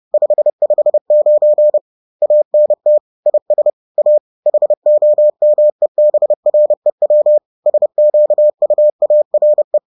Morse code, heard as a keyed tone, about 30 wpm, 600 Hz. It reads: 559 ANT IS A HOMEBREW SQUARE
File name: 30wpm-7